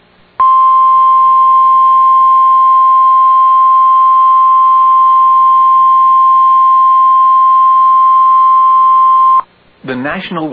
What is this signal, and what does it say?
Emergency Alert System (SAME) tones (the 1050 Hz attention tone), with speech. It is heard from 0.3 to 9.4 s.